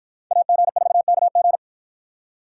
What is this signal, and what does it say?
Morse code, heard as a keyed tone, about 51 wpm, 700 Hz. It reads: AG4KG